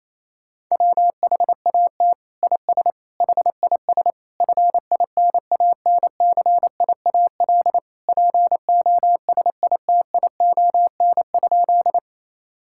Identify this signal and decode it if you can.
Morse code, heard as a keyed tone, about 28 wpm, 700 Hz. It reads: WHAT IS HIS FINANCIAL POSITION?
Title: Sentences from Top 1000 Words - Review Part 5 - 28wpm